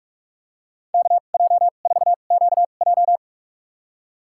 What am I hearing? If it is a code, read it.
Morse code, heard as a keyed tone, about 44 wpm, spaced as for 32 wpm, 700 Hz. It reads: KJ4QJ
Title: USA Call Signs - 2x Character Spacing - 45wpm